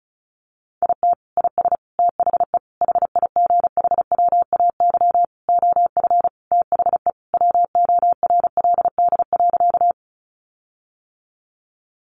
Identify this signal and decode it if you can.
Morse code, heard as a keyed tone, about 34 wpm, 700 Hz. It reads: IT IS THE HIGHWAY OF THE WORLD.
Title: Sentences from Top 1400 Words 35wpm